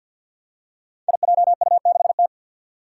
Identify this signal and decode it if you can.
Morse code, heard as a keyed tone, about 52 wpm, 700 Hz. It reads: IJU6T
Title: Five Challenging Letter-Number Combinations 50wpm